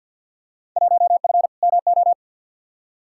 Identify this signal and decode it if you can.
Morse code, heard as a keyed tone, about 51 wpm, 700 Hz. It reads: 1F MO
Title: Field Day Exchanges Encore - 50wpm